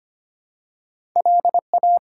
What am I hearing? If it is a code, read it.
Morse code, heard as a keyed tone, about 25 wpm, 700 Hz. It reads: LA
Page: morse-code-ninja-practice-25wpm